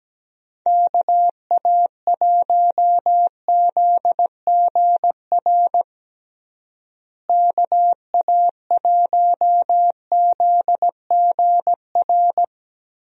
Morse code, heard as a keyed tone, about 17 wpm, 700 Hz. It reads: KA1ZGR KA1ZGR
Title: USA Call Signs - 6 Characters - Repeated 3x - 17wpm